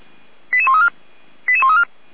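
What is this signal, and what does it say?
Five-tone selective call (ZVEI), heard as quick runs of five tones, tones of about 70 ms. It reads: 88134 88134
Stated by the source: ZVEI five-tone call 88136 received on an SDR.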